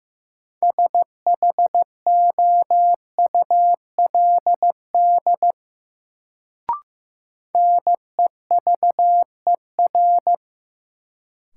Morse code, heard as a keyed tone, about 15 wpm, 700 Hz. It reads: SHOULD NEVER
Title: Sentences from Top 300 Words - New Word Drill - 15wpm